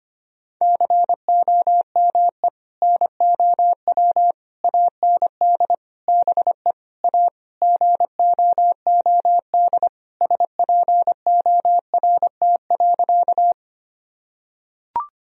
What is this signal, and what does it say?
Morse code, heard as a keyed tone, about 25 wpm, 700 Hz. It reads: COME NOW AND BE A GOOD SPORT.
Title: Sentences from Top 1000 Words - Review Part 4 - 25wpm